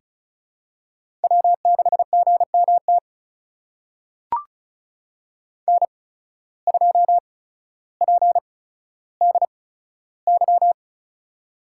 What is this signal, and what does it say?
Morse code, heard as a keyed tone, about 36 wpm, spaced as for 15 wpm, 700 Hz. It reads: W6GMT N2PDY
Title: USA Call Signs - 5 Characters - 8x Character Spacing - 35wpm